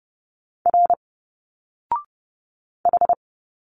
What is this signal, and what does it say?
Morse code, heard as a keyed tone, about 30 wpm, 700 Hz. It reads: R H
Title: Single Letters 30wpm